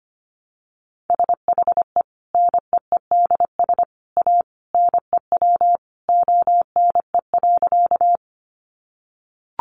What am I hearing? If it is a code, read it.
Morse code, heard as a keyed tone, about 25 wpm, 700 Hz. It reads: SHE NEEDS A NEW ONE.
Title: Additional Easy Phrases 25wpm